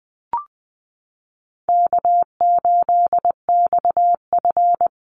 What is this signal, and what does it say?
Morse code, heard as a keyed tone, about 20 wpm, 700 Hz. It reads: K8XF
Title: Easy Call Signs Repeated 2x - 20wpm